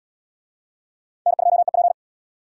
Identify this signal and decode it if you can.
Morse code, heard as a keyed tone, about 66 wpm, 700 Hz. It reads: N2EP